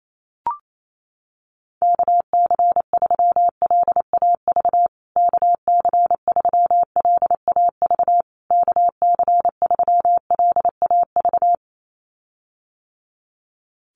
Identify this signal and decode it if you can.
Morse code, heard as a keyed tone, about 28 wpm, 700 Hz. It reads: KC3LAV KC3LAV KC3LAV
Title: USA Call Signs Repeated 3x 28wpm